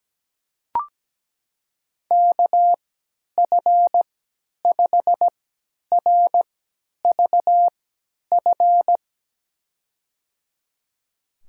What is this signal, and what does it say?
Morse code, heard as a keyed tone, about 17 wpm, spaced as for 10 wpm, 700 Hz. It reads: KF5RVF
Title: USA Call Signs - 3x Character Spacing - 17wpm